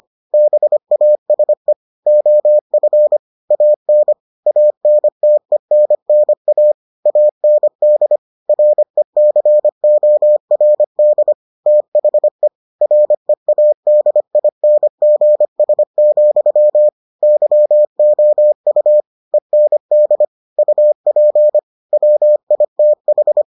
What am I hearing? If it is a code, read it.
Morse code, heard as a keyed tone, about 25 wpm, 600 Hz. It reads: BASE OF AN ANTENNA AND RECORD THE READINGS, YOU END UP WITH